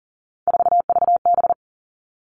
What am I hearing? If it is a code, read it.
Morse code, heard as a keyed tone, about 38 wpm, 700 Hz. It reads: 4VB
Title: Three Challenging Letter-Number Combinations 40wpm